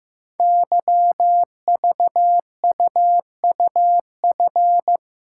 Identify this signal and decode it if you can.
Morse code, heard as a keyed tone, about 15 wpm, 700 Hz. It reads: YVUUF
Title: Code Groups - 5 Letter:Numbers - 15wpm